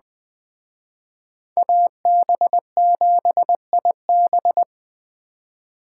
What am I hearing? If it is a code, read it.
Morse code, heard as a keyed tone, about 20 wpm, 700 Hz. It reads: AB7IB